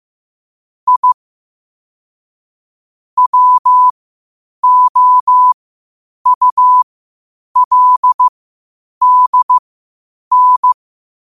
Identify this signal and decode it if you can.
Morse code, heard as a keyed tone, about 15 wpm, spaced as for 8 wpm, 1000 Hz. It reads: I WOULDN
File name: qotdfiles / 8wpm